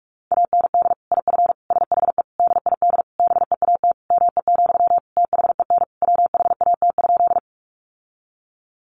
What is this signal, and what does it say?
Morse code, heard as a keyed tone, about 44 wpm, 700 Hz. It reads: AND IF SHE DID BEAT ME, THEN WHAT?
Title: Sentences from Top 1300 Words 45wpm